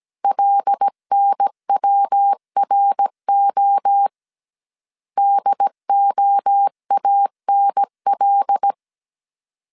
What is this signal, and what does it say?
Morse code, heard as a keyed tone, about 17 wpm, 800 Hz. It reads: LNWRO DOANL
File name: 16wpm